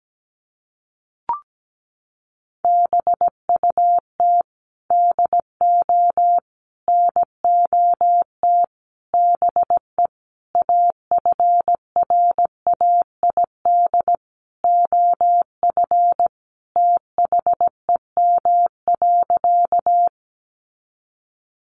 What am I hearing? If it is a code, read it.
Morse code, heard as a keyed tone, about 17 wpm, 700 Hz. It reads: BUT DO NOT BE AFRAID OF THEM.
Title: Sentences from Top 1500 Words 17wpm